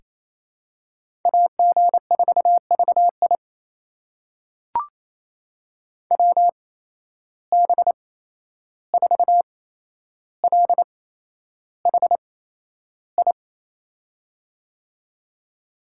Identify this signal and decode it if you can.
Morse code, heard as a keyed tone, about 28 wpm, spaced as for 11 wpm, 700 Hz. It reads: AG4VI WB4LHI
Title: USA Call Signs - 8x Character Spacing - 28wpm